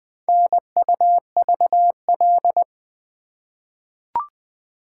Morse code, heard as a keyed tone, about 20 wpm, 700 Hz. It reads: NUVL
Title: Four Letter-Number Combinations - 20wpm